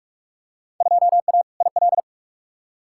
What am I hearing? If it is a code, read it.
Morse code, heard as a keyed tone, about 47 wpm, 700 Hz. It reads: 2A IL
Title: Field Day Exchanges - 45wpm